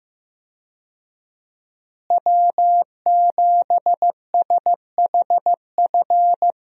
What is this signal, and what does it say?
Morse code, heard as a keyed tone, about 15 wpm, 700 Hz. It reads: W7SHF
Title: USA Call Signs - 5 Characters - Repeated 2x - 15wpm